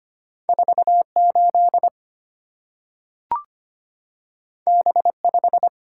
Morse code, heard as a keyed tone, about 25 wpm, 700 Hz. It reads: 48 B5
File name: Two Letter-Number Combinations - 25wpm